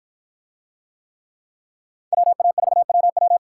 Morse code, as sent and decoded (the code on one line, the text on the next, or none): .-- .- ....- .-- .--
WA4WW